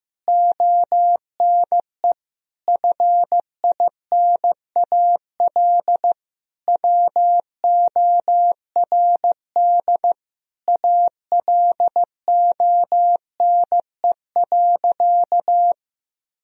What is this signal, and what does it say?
Morse code, heard as a keyed tone, about 15 wpm, 700 Hz. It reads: ONE FINAL WORD ALONE.
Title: Sentences from Top 1000 Words - Review Part 2 - 15wpm